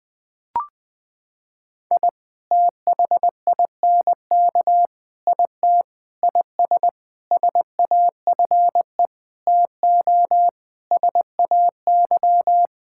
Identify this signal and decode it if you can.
Morse code, heard as a keyed tone, about 20 wpm, 700 Hz. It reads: I THINK IT IS SAFE TO SAY
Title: Sentences with 7-Word Phrases - 20wpm